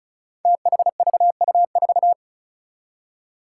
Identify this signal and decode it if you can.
Morse code, heard as a keyed tone, about 36 wpm, 700 Hz. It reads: THVU4
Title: Five Challenging Letter-Number Combinations 35wpm